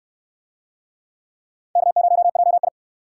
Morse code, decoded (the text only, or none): K0JI